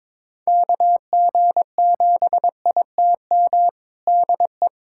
Morse code, heard as a keyed tone, about 22 wpm, 700 Hz. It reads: KG7ITM DE
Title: USA Callsign DE Callsign 22wpm